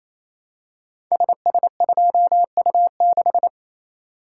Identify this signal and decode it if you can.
Morse code, heard as a keyed tone, about 28 wpm, 700 Hz. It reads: SS2U6
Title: Code Groups - 5 Letter:Numbers - 28wpm